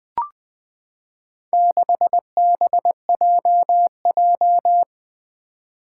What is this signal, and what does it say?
Morse code, heard as a keyed tone, about 20 wpm, 700 Hz. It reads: 6BJJ
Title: Four Challenging Letter-Number Combinations 20wpm